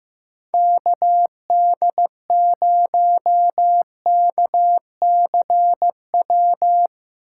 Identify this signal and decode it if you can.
Morse code, heard as a keyed tone, about 17 wpm, 700 Hz. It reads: KD0KCW
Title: Speed Racing - USA 2x3 Call Signs 15wpm